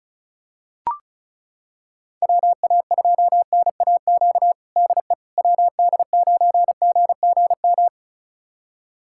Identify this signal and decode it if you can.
Morse code, heard as a keyed tone, about 36 wpm, 700 Hz. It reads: WA2NAQ DE WD9GGM
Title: USA Callsign DE Callsign - 6 Characters - 35wpm